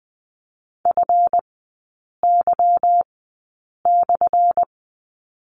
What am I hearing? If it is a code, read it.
Morse code, heard as a keyed tone, about 20 wpm, spaced as for 8 wpm, 700 Hz. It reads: FY/
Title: 184 Lesson 30e - Letter b - Three Characters 20wpm